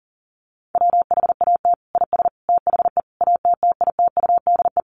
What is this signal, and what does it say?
Morse code, heard as a keyed tone, about 40 wpm, 700 Hz. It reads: WHAT IS THE ATTITUDE
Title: Sentences from Top 1300 Words 40wpm